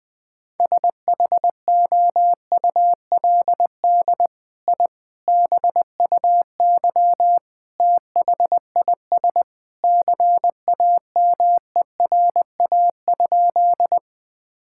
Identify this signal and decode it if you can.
Morse code, heard as a keyed tone, about 21 wpm, 700 Hz. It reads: SHOULD I BUY THIS CAMERA?
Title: Sentences from Top 1000 Words - Review Part 5 - 20wpm